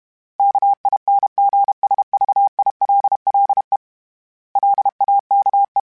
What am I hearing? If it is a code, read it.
Morse code, heard as a keyed tone, about 32 wpm, 800 Hz. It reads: KINGSVILLE LAKE
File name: fromSMs / FromSMs mp3s / Cities@30